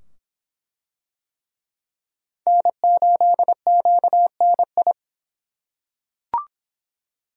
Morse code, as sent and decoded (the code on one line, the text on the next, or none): -. ---.. --.- -. ..
N8QNI